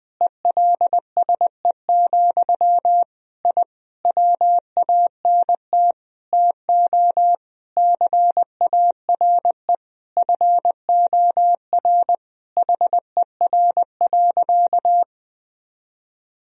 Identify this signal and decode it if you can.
Morse code, heard as a keyed tone, about 20 wpm, 700 Hz. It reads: ELSE, I WANT TO CARE FOR HER.
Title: Sentences from Top 1000 Words - Review Part 2 - 20wpm